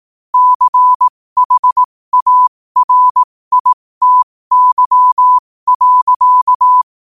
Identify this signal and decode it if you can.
Morse code, heard as a keyed tone, about 18 wpm, spaced as for 15 wpm, 1000 Hz. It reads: CHARITY.
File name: 15wpm